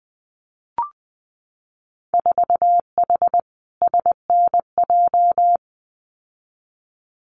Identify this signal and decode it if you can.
Morse code, heard as a keyed tone, about 20 wpm, 700 Hz. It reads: 4H SNJ
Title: Winter Field Day Exchanges Encore 20wpm